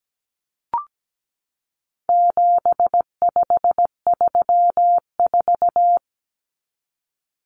Morse code, heard as a keyed tone, about 17 wpm, 700 Hz. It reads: 7534
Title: Common Cut Numbers - 4-Digit 17wpm